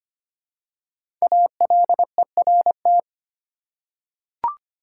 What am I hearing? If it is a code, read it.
Morse code, heard as a keyed tone, about 25 wpm, 700 Hz. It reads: ALERT
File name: 5 Letter Words - First 3 Letters Given - 25wpm